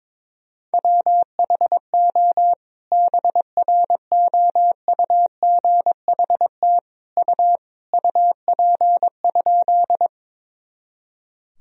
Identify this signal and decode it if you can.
Morse code, heard as a keyed tone, about 22 wpm, 700 Hz. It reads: WHO BROUGHT U UP?
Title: Sentences from Top 500 Words and Abbreviations 22wpm